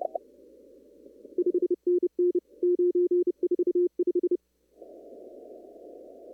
Morse code, heard as a keyed tone, about 30 wpm, 360 Hz. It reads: 5NN 945
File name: CW / kt5j wpx cw-op-k5pi